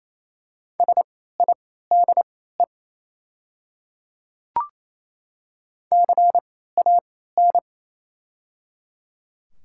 Morse code, heard as a keyed tone, about 28 wpm, spaced as for 16 wpm, 700 Hz. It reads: SIDE CAN
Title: 3x Character Spacing - Top 100 Words - 28wpm